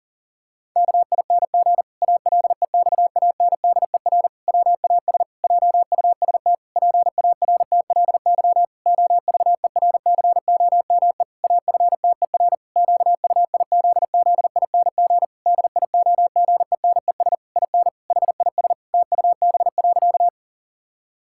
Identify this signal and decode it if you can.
Morse code, heard as a keyed tone, about 40 wpm, 700 Hz. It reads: KING ALEXANDER WAS JUST PARTLY OVERCOME AFTER QUIZZING DIOGENES IN HIS TUB.
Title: Pangram Challenge v2 - 40wpm